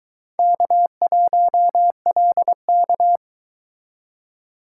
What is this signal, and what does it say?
Morse code, heard as a keyed tone, about 24 wpm, 700 Hz. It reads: K1LK